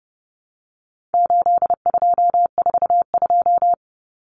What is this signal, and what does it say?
Morse code, heard as a keyed tone, about 30 wpm, 700 Hz. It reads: 8242
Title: Common Cut Numbers - 4-Digit 30wpm